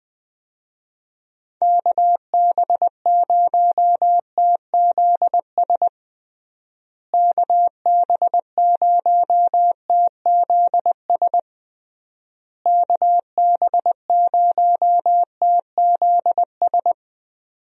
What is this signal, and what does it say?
Morse code, heard as a keyed tone, about 20 wpm, 700 Hz. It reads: KB0TZS KB0TZS KB0TZS
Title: USA Call Signs - 6 Characters - Repeated 3x - 20wpm